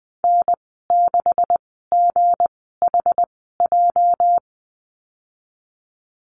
Morse code, heard as a keyed tone, about 20 wpm, spaced as for 15 wpm, 700 Hz. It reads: N6GHJ
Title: 213 Lesson 34g - Number 8 - Call Signs 20wpm